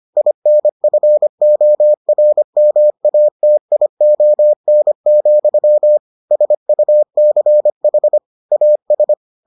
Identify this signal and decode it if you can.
Morse code, heard as a keyed tone, about 25 wpm, 600 Hz. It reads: INFORMATION, SUCH AS